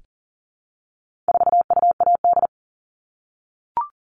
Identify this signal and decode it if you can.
Morse code, heard as a keyed tone, about 40 wpm, 700 Hz. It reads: 4UAD
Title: All Cut Numbers - 4-Digit 40wpm